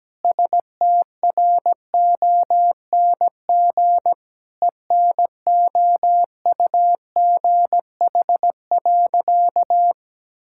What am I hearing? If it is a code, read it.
Morse code, heard as a keyed tone, about 17 wpm, 700 Hz. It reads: STRONG ENOUGH.
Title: Sentences from Top 1600 Words 17wpm